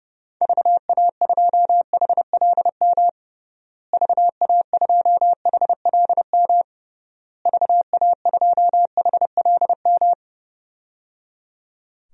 Morse code, heard as a keyed tone, about 30 wpm, 700 Hz. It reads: VA2HLM VA2HLM VA2HLM
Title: DXCC-100 Call Signs - 6 Characters - Repeated 3x - 30wpm